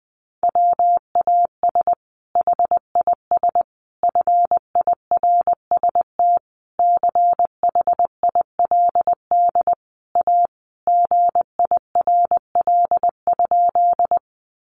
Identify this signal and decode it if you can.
Morse code, heard as a keyed tone, about 20 wpm, 700 Hz. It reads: WAS HIS FIRST CHILD A GIRL?
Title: Sentences from Top 800 Words 20wpm